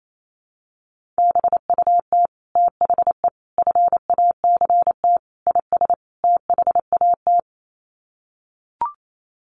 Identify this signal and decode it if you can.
Morse code, heard as a keyed tone, about 28 wpm, 700 Hz. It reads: BUT THE FACT IS THAT
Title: Sets of 5 Words Encore2 28wpm